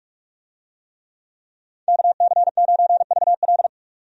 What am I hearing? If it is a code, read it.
Morse code, heard as a keyed tone, about 44 wpm, 700 Hz. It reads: KC9VL
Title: USA Call Signs - 5 Characters - 45wpm